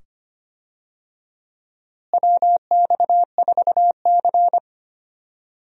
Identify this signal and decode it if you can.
Morse code, heard as a keyed tone, about 25 wpm, 700 Hz. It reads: WX4C
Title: USA Call Signs - 4 Characters - Repeated 2x - 25wpm